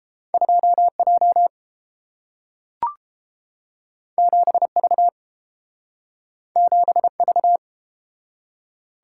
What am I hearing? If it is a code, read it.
Morse code, heard as a keyed tone, about 31 wpm, 700 Hz. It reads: 2J 7V 7V